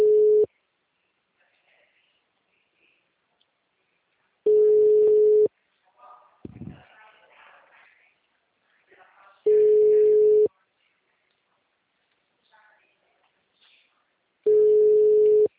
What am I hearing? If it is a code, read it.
Ringback tone (Europe, 425 Hz), with speech.